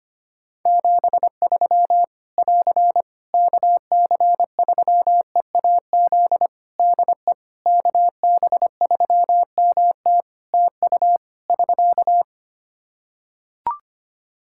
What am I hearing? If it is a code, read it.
Morse code, heard as a keyed tone, about 25 wpm, 700 Hz. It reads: 73 <AR> KC3EAZ DE KB3MT TU <SK>
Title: Practice QSO - Exchange 4 - Ending - 25wpm